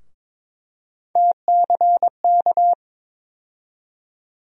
Morse code, heard as a keyed tone, about 22 wpm, 700 Hz. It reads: TCK
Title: USA Call Sign Postfixes - 3 Letters 22wpm